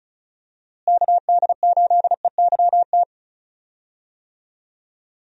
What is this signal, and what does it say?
Morse code, heard as a keyed tone, about 34 wpm, 700 Hz. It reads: KD8EYT
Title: USA Call Signs - 6 Characters - Repeated 3x - 35wpm